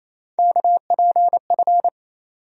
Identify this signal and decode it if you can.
Morse code, heard as a keyed tone, about 27 wpm, 700 Hz. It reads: KPF
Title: USA Call Sign Postfixes - 3 Letters 28wpm